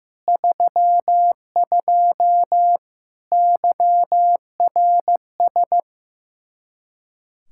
Morse code, heard as a keyed tone, about 15 wpm, 700 Hz. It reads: 32 YRS
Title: Practice QSO - Exchange 3 - More Info - 15wpm